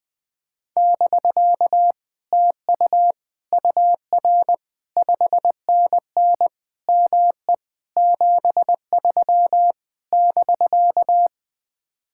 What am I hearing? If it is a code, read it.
Morse code, heard as a keyed tone, about 20 wpm, 700 Hz. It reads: <BK> TU UR 5NN ME 73 <BK>
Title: Speed-Racing Common QSO Phrases Encore - 17wpm